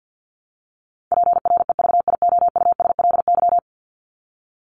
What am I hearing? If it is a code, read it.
Morse code, heard as a keyed tone, about 52 wpm, 700 Hz. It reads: PREVIOUSLY
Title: LY Suffix - Top 100 Words with Base Words 50wpm